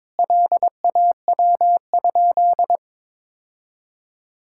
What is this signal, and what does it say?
Morse code, heard as a keyed tone, about 22 wpm, 700 Hz. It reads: LAW?
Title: Sentences from Top 1600 Words 22wpm